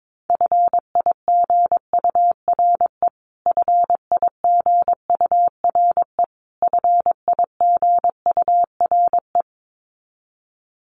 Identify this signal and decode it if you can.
Morse code, heard as a keyed tone, about 22 wpm, 700 Hz. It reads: FIGURE FIGURE FIGURE
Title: Top 500 Words Repeated 22wpm